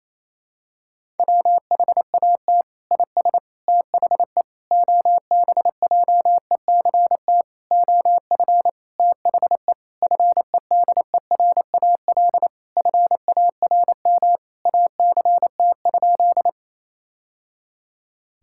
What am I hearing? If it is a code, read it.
Morse code, heard as a keyed tone, about 27 wpm, 700 Hz. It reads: WHAT IS THE OBJECT OF THE FEDERAL FARM ACT?
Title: Sentences from Top 1000 Words - Review Part 5 - 28wpm